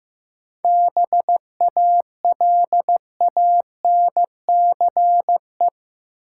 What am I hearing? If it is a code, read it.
Morse code, heard as a keyed tone, about 15 wpm, 700 Hz. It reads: BALANCE
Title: Sentences from Top 900 Words - New Word Drill - 15wpm